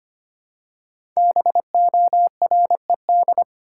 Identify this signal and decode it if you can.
Morse code, heard as a keyed tone, about 25 wpm, 700 Hz. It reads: BORED
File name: 5 Letter Words - First 3 Letters Given - 25wpm